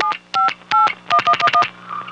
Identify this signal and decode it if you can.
DTMF, heard as telephone touch tones, tones of about 90 ms. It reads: 7581111